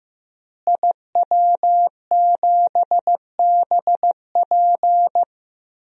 Morse code, heard as a keyed tone, about 15 wpm, 700 Hz. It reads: IW7BP
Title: DXCC-100 Call Signs - 5 Characters - Repeated 3x - 15wpm